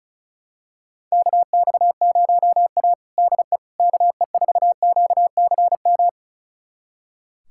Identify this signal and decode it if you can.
Morse code, heard as a keyed tone, about 34 wpm, 700 Hz. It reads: KX0A DE KE4QCM
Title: USA Callsign DE Callsign 35wpm